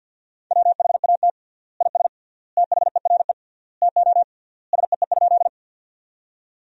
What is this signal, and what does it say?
Morse code, heard as a keyed tone, about 50 wpm, 700 Hz. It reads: WHAT IS THERE TO SEE?
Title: 3x Word-Spacing - Sentences from Top 100 Words - 50wpm